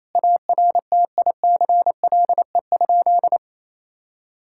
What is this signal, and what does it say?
Morse code, heard as a keyed tone, about 27 wpm, 700 Hz. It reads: ARTICLE?
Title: Sentences from Top 1000 Words - Review Part 5 - 28wpm